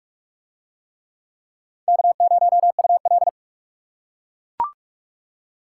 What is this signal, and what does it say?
Morse code, heard as a keyed tone, about 40 wpm, 700 Hz. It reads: K0UL